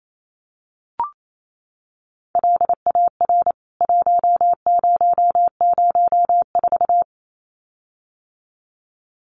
Morse code, heard as a keyed tone, about 28 wpm, 700 Hz. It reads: LAR 1004
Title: CWT Exchange 28wpm